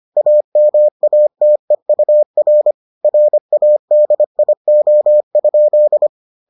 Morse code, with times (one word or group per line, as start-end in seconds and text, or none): 0.2-2.7 AMATEUR
3.0-6.1 RADIO?